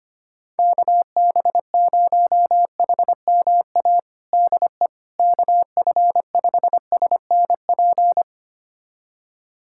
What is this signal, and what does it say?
Morse code, heard as a keyed tone, about 25 wpm, 700 Hz. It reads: KB0HMA DE KF5SNP
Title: USA Callsign DE Callsign - 6 Characters - 25wpm